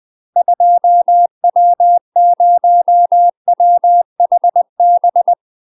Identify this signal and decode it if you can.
Morse code, heard as a keyed tone, about 20 wpm, 700 Hz. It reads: 2W0WHB
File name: UK MASTER QSOs TRAINING / 20WPM / UK MASTER QSOs 0010